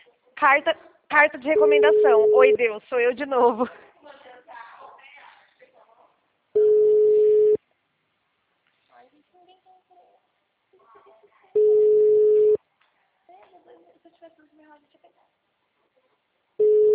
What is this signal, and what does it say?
Ringback tone (Europe, 425 Hz), with speech.